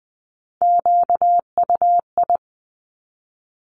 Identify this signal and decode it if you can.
Morse code, heard as a keyed tone, about 20 wpm, 700 Hz. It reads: QUI
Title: Top Three Letter Combinations in Words 20wpm